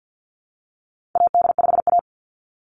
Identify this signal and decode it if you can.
Morse code, heard as a keyed tone, about 50 wpm, 700 Hz. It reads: AD5A
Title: NA SOTA Chaser Callsigns 50wpm